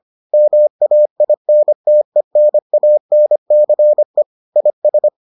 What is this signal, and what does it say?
Morse code, heard as a keyed tone, about 25 wpm, 600 Hz. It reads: MAINTENANCE IS